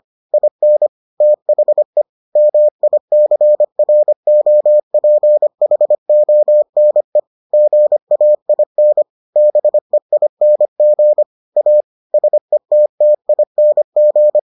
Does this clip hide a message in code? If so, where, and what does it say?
Yes: Morse code from 0.3 to 14.4 s, heard as a keyed tone, about 25 wpm, 600 Hz. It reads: IN THE MICROPHONE GAIN BEING A SETTING